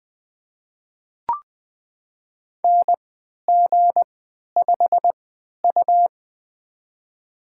Morse code, heard as a keyed tone, about 20 wpm, spaced as for 11 wpm, 700 Hz. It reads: NG5U
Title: USA Call Signs - 4 Characters - 3x Character Spacing - 20wpm